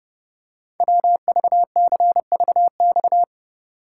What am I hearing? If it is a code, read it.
Morse code, heard as a keyed tone, about 29 wpm, 700 Hz. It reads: WVCVX
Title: Code Groups - 5 Letter:Numbers - 30wpm